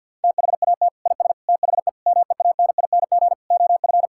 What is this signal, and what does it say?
Morse code, heard as a keyed tone, about 50 wpm, 700 Hz. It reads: THAT IS THE MEANING OF
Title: Sentences from Top 1000 Words - Review Part 6 - 50wpm